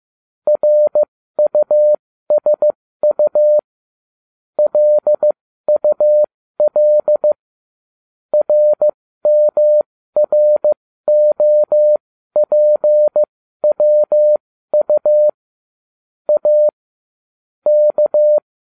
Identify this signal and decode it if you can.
Morse code, heard as a keyed tone, about 15 wpm, spaced as for 13 wpm, 600 Hz. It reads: RUSU LUL RMROPWU A K